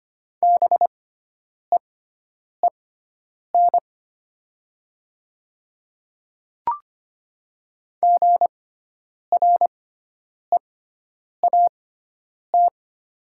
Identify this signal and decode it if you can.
Morse code, heard as a keyed tone, about 25 wpm, spaced as for 9 wpm, 700 Hz. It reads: BEEN GREAT
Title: 6x Character Spacing - Top 200 Words - 25wpm